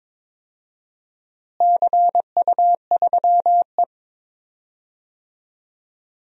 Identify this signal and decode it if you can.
Morse code, heard as a keyed tone, about 22 wpm, 700 Hz. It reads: CU3E
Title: DXCC-100 Call Signs - 4 Characters - Repeated 3x - 22wpm